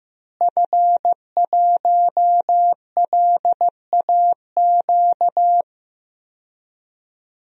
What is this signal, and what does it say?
Morse code, heard as a keyed tone, about 15 wpm, 700 Hz. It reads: F1LAQ
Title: DXCC-100 Call Signs - 5 Characters - Repeated 2x - 15wpm